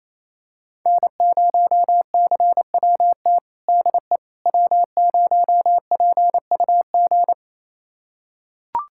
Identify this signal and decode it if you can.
Morse code, heard as a keyed tone, about 28 wpm, 700 Hz. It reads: N0CWT DE W0PUG
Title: USA Callsign DE Callsign - 5 Characters - 28wpm